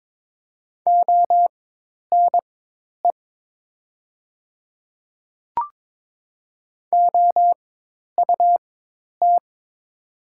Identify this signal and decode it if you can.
Morse code, heard as a keyed tone, about 22 wpm, spaced as for 10 wpm, 700 Hz. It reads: ONE OUT
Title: 4x Character Spacing - Top 100 Words - 22wpm